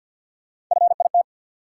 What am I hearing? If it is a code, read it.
Morse code, heard as a keyed tone, about 53 wpm, 700 Hz. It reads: FIT
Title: Sentences from Top 800 Words - New Word Drill - 50wpm